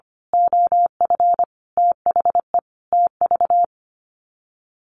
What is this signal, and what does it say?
Morse code, heard as a keyed tone, about 25 wpm, 700 Hz. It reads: OF THE TV
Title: Sets of 5 Words Encore2 25wpm